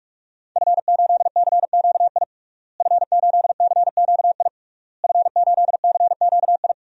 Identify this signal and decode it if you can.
Morse code, heard as a keyed tone, about 41 wpm, 700 Hz. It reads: F8CQI F8CQI F8CQI
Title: DXCC-100 Call Signs - 5 Characters - Repeated 3x - 45wpm